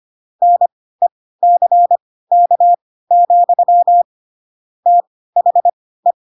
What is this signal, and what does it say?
Morse code, heard as a keyed tone, about 25 wpm, spaced as for 16 wpm, 700 Hz. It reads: NECK, THE
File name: Pinocchio / Pinocchio ch3 10 25 ik1hge